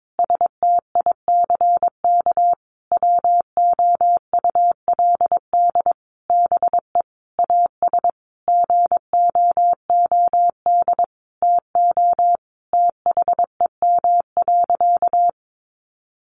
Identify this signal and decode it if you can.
Morse code, heard as a keyed tone, about 22 wpm, 700 Hz. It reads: STICK WOULD BE AS GOOD TO THEM.
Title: Sentences from Top 1200 Words 22wpm